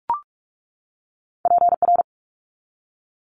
Morse code, heard as a keyed tone, about 46 wpm, 700 Hz. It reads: PR
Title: ARRL Sections 45wpm